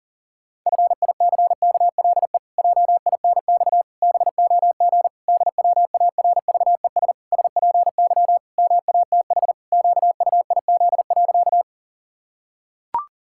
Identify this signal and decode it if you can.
Morse code, heard as a keyed tone, about 40 wpm, 700 Hz. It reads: FICKLE JINX BOG DWARVES SPY MATH QUIZ.
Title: Pangram Challenge v2 - 40wpm